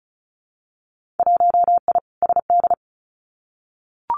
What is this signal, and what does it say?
Morse code, heard as a keyed tone, about 36 wpm, 700 Hz. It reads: 1I SD
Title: Winter Field Day Exchanges 35wpm